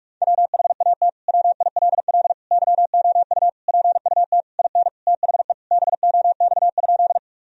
Morse code, heard as a keyed tone, about 46 wpm, 700 Hz. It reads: WHAT WILL YOU PUT IN THE BOX?
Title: Sentences from Top 1000 Words - Review Part 2 - 45wpm